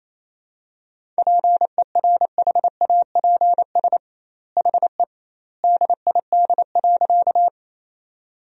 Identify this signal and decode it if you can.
Morse code, heard as a keyed tone, about 28 wpm, 700 Hz. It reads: PERHAPS HE DID.
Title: 2x Word-Spacing - Sentences from Top 500 Words - 28wpm